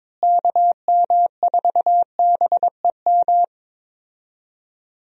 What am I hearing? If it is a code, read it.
Morse code, heard as a keyed tone, about 22 wpm, 700 Hz. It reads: KM4BEM
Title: USA Call Signs - 6 Characters - Repeated 2x - 22wpm